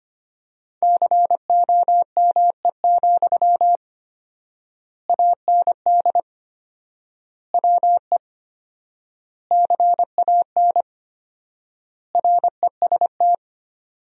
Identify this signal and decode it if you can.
Morse code, heard as a keyed tone, about 25 wpm, 700 Hz. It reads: COME, AND WE CAN REST
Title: 4x Word-Spacing - Sentences from Top 500 Words - 25wpm